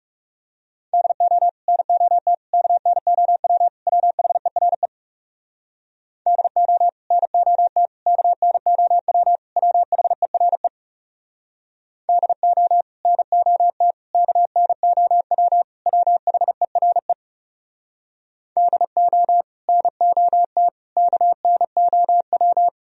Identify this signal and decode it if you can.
Morse code, heard as a keyed tone, about 37 wpm, 700 Hz. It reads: DO NOT KNOW WHERE DO NOT KNOW WHERE DO NOT KNOW WHERE DO NOT KNOW
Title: Speed-Racing - Sets of 4 Words 30wpm